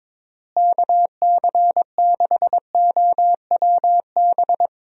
Morse code, heard as a keyed tone, about 22 wpm, 700 Hz. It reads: KC6OWB
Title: USA Call Signs - 6 Characters - Repeated 3x - 22wpm